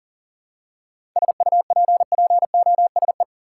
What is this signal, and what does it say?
Morse code, heard as a keyed tone, about 40 wpm, 700 Hz. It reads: SUPPOSE
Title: Seven Letter Words - 40wpm